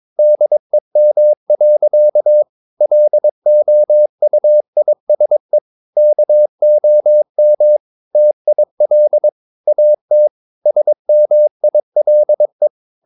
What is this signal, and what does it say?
Morse code, heard as a keyed tone, about 22 wpm, 600 Hz. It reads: DEM. LOUISE KOM TIL AT SMILE
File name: Ferie QRS 22wpm